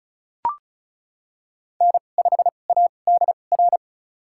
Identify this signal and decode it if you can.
Morse code, heard as a keyed tone, about 34 wpm, spaced as for 25 wpm, 700 Hz. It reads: N5ADR
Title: USA Call Signs - 2x Character Spacing - 35wpm